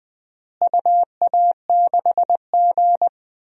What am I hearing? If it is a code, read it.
Morse code, heard as a keyed tone, about 20 wpm, 700 Hz. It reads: UA6G
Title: DXCC-100 Call Signs - 4 Characters - Repeated 3x - 20wpm